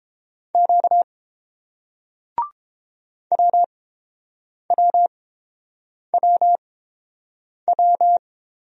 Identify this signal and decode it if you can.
Morse code, heard as a keyed tone, about 26 wpm, spaced as for 7 wpm, 700 Hz. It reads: Q WWWW
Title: Speed-Racing Single Letters 22wpm